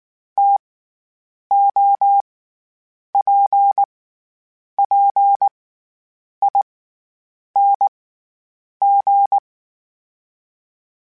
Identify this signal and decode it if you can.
Morse code, heard as a keyed tone, about 19 wpm, 800 Hz. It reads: T O P P I N G
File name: Variable Speed / Variable Speed-mp3s / TO sp @18